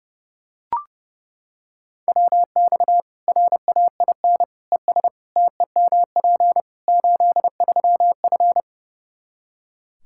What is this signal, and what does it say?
Morse code, heard as a keyed tone, about 30 wpm, 700 Hz. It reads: WX RAIN ES TEMP 83F
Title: Practice QSO - Exchange 2 - Info - 30wpm